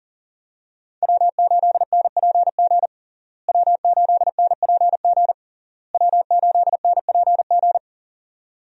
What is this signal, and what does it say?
Morse code, heard as a keyed tone, about 40 wpm, 700 Hz. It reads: W8NPG W8NPG W8NPG
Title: USA Call Signs - 5 Characters - Repeated 3x - 40wpm